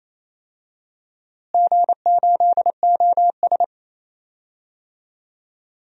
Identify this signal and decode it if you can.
Morse code, heard as a keyed tone, about 28 wpm, 700 Hz. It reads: G8OS
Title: DXCC-100 Call Signs - 4 Characters - Repeated 2x - 28wpm